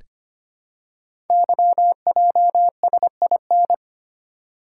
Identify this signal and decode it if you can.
Morse code, heard as a keyed tone, about 25 wpm, 700 Hz. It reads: YJSIN
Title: Code Groups - 5 Letters - 25wpm